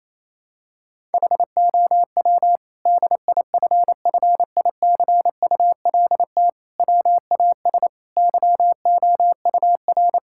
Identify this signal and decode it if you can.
Morse code, heard as a keyed tone, about 28 wpm, 700 Hz. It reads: HOW DIFFICULT WAS YOUR
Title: Sentences from Top 1000 Words - Review Part 6 - 28wpm